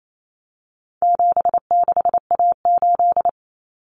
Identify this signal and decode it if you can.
Morse code, heard as a keyed tone, about 28 wpm, 700 Hz. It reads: 76A8
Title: Common Cut Numbers - 4-Digit 28wpm